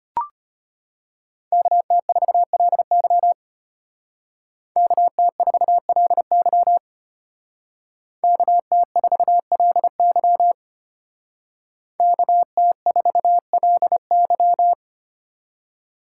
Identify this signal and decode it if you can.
Morse code, heard as a keyed tone, about 33 wpm, 700 Hz. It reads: KT4LY KT4LY KT4LY KT4LY